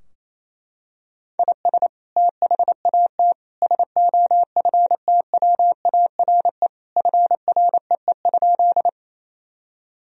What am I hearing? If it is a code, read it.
Morse code, heard as a keyed tone, about 27 wpm, 700 Hz. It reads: IS THAT SOFTWARE FREE?
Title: Sentences from Top 1000 Words - Review Part 3 - 28wpm